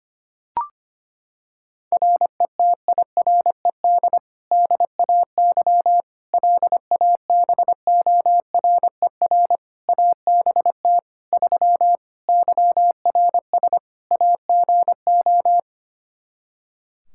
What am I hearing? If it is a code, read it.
Morse code, heard as a keyed tone, about 25 wpm, 700 Hz. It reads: RETIRED DAY LABORER ABT 3 YRS AGO
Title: Practice QSO - Exchange 3 - More Info - 25wpm